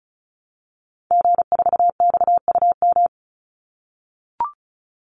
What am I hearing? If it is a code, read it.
Morse code, heard as a keyed tone, about 34 wpm, 700 Hz. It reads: G4XUM
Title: Call Signs - 35wpm